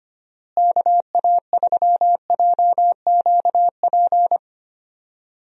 Morse code, heard as a keyed tone, about 25 wpm, 700 Hz. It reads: KA3JQP
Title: USA Call Signs - 6 Characters - Repeated 3x - 25wpm